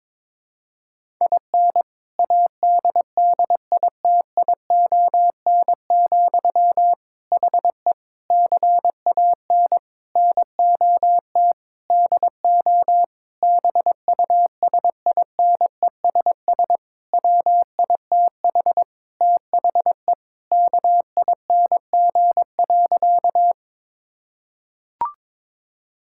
Morse code, heard as a keyed tone, about 22 wpm, 700 Hz. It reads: IN ADDITION, HE CAN NOT DO BUSINESS WITH THE KING.
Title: Sentences from Top 1000 Words - Review Part 6 - 22wpm